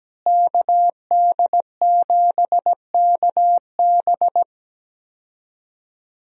Morse code, heard as a keyed tone, about 17 wpm, 700 Hz. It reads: KD7KB
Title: USA Call Signs - 5 Characters - Repeated 2x - 17wpm